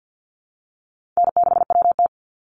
Morse code, heard as a keyed tone, about 50 wpm, 700 Hz. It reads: N6WT
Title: NA SOTA Chaser Callsigns 50wpm